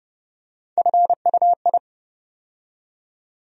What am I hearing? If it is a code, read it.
Morse code, heard as a keyed tone, about 30 wpm, 700 Hz. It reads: FUI
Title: USA Call Sign Postfixes - 3 Letters 30wpm